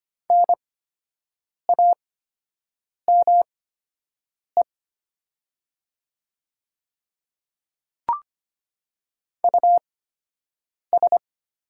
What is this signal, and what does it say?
Morse code, heard as a keyed tone, about 25 wpm, spaced as for 7 wpm, 700 Hz. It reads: NAME US
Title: 8x Character Spacing - Top 200 Words - 25wpm